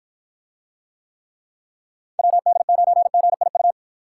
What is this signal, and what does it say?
Morse code, heard as a keyed tone, about 52 wpm, 700 Hz. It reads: WD9GIU